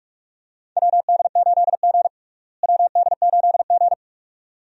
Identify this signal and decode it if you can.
Morse code, heard as a keyed tone, about 43 wpm, 700 Hz. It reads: WD8G WD8G
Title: USA Call Signs - 4 Characters - Repeated 2x - 45wpm